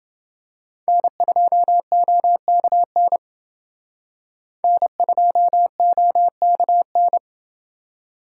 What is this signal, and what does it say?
Morse code, heard as a keyed tone, about 30 wpm, 700 Hz. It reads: N2OKN N2OKN